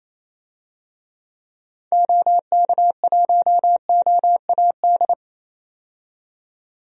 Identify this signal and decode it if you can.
Morse code, heard as a keyed tone, about 28 wpm, 700 Hz. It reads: OK1OAD
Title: DXCC-100 Call Signs - 6 Characters - 28wpm